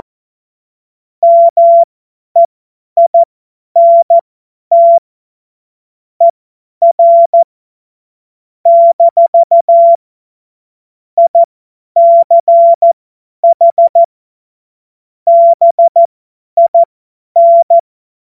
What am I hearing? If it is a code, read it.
Morse code, heard as a keyed tone, about 14 wpm, spaced as for 10 wpm, 680 Hz. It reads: MEINT ER - ICH BIN
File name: Morsekurs / Witze / Witz 08. August 2023 50 F70